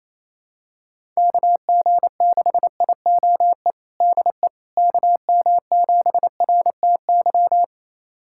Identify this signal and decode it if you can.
Morse code, heard as a keyed tone, about 28 wpm, 700 Hz. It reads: KG6IOE DE KM7RTY
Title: USA Callsign DE Callsign 28wpm